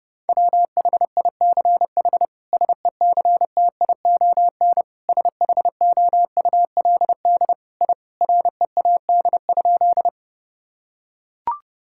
Morse code, heard as a keyed tone, about 30 wpm, 700 Hz. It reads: WHICH SECTION SHOULD I READ?
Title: Sentences from Top 1000 Words - Review Part 5 - 30wpm